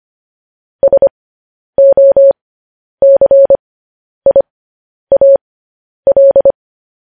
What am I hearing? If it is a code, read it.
Morse code, heard as a keyed tone, about 25 wpm, spaced as for 10 wpm, 550 Hz. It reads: SOCIAL
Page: news-headlines-in-morse-code-at-10-wpm